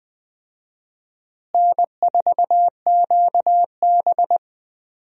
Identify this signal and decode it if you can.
Morse code, heard as a keyed tone, about 20 wpm, 700 Hz. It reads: N4QB
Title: USA Call Signs - 4 Characters - Repeated 2x - 20wpm